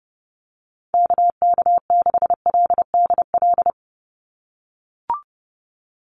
Morse code, heard as a keyed tone, about 30 wpm, 700 Hz. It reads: KK6LDL
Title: USA Call Signs 30wpm